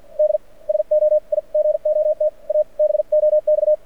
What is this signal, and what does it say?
Morse code, heard as a keyed tone, about 50 wpm, 600 Hz. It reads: D SO I GOT A BOX
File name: archived / CRSnet